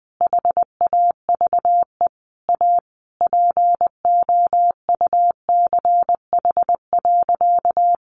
Morse code, heard as a keyed tone, about 19 wpm, 700 Hz. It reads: HAVE A POUCH.
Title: Easy Phrases 20wpm